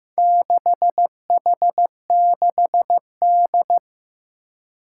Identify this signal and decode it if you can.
Morse code, heard as a keyed tone, about 15 wpm, 700 Hz. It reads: 6H6D
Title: Four Challenging Letter-Number Combinations 15wpm